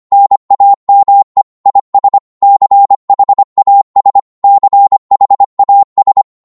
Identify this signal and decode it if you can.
Morse code, heard as a keyed tone, about 25 wpm, 800 Hz. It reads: NAME IS CHAS CHAS
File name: Morse / qso-25wpm